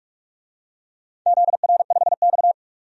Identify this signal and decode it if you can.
Morse code, heard as a keyed tone, about 42 wpm, 700 Hz. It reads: ZR5X
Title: DXCC-100 Call Signs - 4 Characters - 45wpm